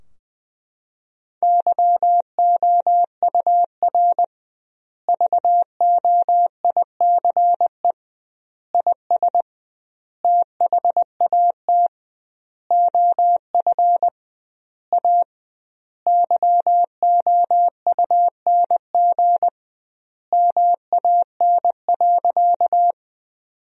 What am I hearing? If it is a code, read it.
Morse code, heard as a keyed tone, about 20 wpm, 700 Hz. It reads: YOUR VOICE IS THAT OF A YOUNG MAN.
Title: 2x Word-Spacing - Sentences from Top 500 Words - 20wpm